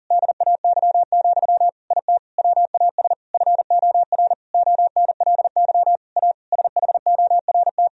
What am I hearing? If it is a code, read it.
Morse code, heard as a keyed tone, about 37 wpm, 700 Hz. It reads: DAY, IT WAS FOR ONLY A SHORT
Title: Sentences from Top 1000 Words - Review Part 3 - 40wpm